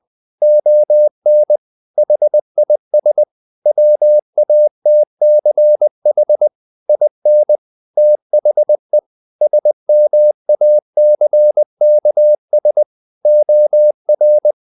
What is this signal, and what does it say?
Morse code, heard as a keyed tone, about 20 wpm, 600 Hz. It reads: ON HIS WATCH IN THE SMACKS OR
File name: the old man and the sea / the old man and the sea part 1 20wpm